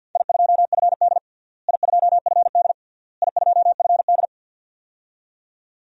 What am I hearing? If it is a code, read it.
Morse code, heard as a keyed tone, about 50 wpm, 700 Hz. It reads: I2FD I2FD I2FD
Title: DXCC-100 Call Signs - 4 Characters - Repeated 3x - 50wpm